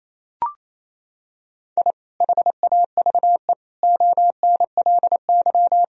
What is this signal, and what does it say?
Morse code, heard as a keyed tone, about 28 wpm, 700 Hz. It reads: I HAVE ONLY
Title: Sentences from Top 1000 Words - Review Part 1 - 28wpm